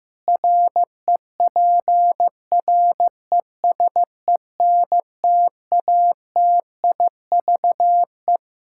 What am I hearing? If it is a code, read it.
Morse code, heard as a keyed tone, about 16 wpm, 700 Hz. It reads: REPRESENTATIVE
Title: Practice QSO - Exchange 3 - More Info - 15wpm